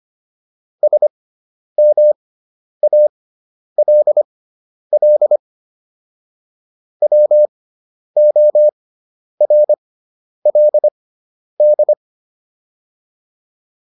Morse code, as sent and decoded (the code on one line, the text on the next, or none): ... -- .- .-.. .-.. / .-- --- .-. .-.. -..
SMALL WORLD